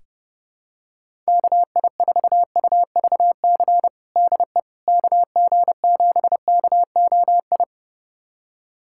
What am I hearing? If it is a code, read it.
Morse code, heard as a keyed tone, about 30 wpm, 700 Hz. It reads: KI4UVC DE KG7KOI
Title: USA Callsign DE Callsign - 6 Characters - 30wpm